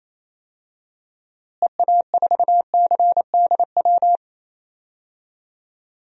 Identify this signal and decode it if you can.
Morse code, heard as a keyed tone, about 28 wpm, 700 Hz. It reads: EA4CDW
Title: DXCC-100 Call Signs - 6 Characters - Repeated 3x - 28wpm